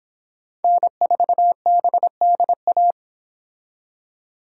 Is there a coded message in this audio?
Yes: Morse code from 0.6 to 2.9 s, heard as a keyed tone, about 31 wpm, 700 Hz. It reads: N4BDA